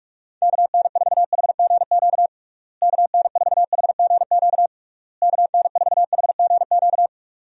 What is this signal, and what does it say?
Morse code, heard as a keyed tone, about 46 wpm, 700 Hz. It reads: KN4HGQ KN4HGQ KN4HGQ
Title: USA Call Signs - 6 Characters - Repeated 3x - 45wpm